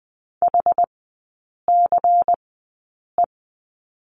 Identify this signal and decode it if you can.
Morse code, heard as a keyed tone, about 20 wpm, spaced as for 8 wpm, 700 Hz. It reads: HCE
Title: 127 Lesson 21e - Number 6 - Three Characters 20wpm